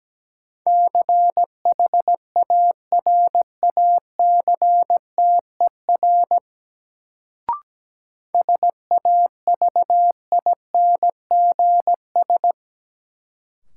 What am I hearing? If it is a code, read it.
Morse code, heard as a keyed tone, about 17 wpm, 700 Hz. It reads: CHARACTER SAVINGS
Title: Sentences from Top 1200 Words - New Word Drill Alt - 17wpm